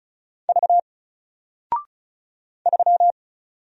Morse code, heard as a keyed tone, about 36 wpm, 700 Hz. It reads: V 3
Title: Single Letter-Number Alt 35wpm